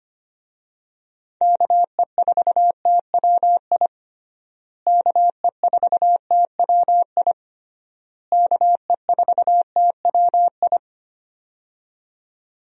Morse code, heard as a keyed tone, about 25 wpm, 700 Hz. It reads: KE4TWI KE4TWI KE4TWI
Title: USA Call Signs - 6 Characters - Repeated 3x - 25wpm